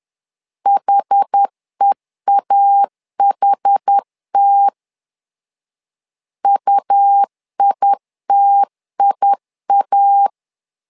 Morse code, heard as a keyed tone, about 10 wpm, 800 Hz. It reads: HEAHT UITIA
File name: LEZIONI / 2 UHDN / 2 UHD 10wpm 1826